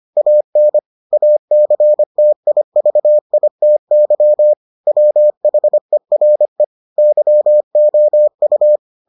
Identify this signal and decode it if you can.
Morse code, heard as a keyed tone, about 25 wpm, 600 Hz. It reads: AN ACTIVITY WHERE YOU